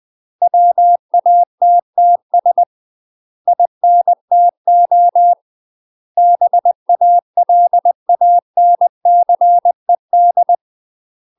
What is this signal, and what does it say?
Morse code, heard as a keyed tone, about 20 wpm, 700 Hz. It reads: WATTS INTO BALANCED
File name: UK MASTER QSOs TRAINING / 20WPM / UK MASTER QSOs 0042